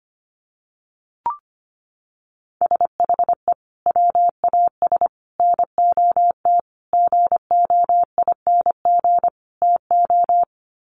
Morse code, heard as a keyed tone, about 25 wpm, 700 Hz. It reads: SHE WAS NOT GOING TO
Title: Sets of 5 Words Encore2 25wpm